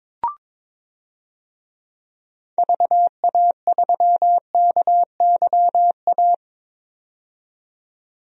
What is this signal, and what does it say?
Morse code, heard as a keyed tone, about 21 wpm, 700 Hz. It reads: VA3KYA
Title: DXCC-100 Call Signs - 6 Characters - 22wpm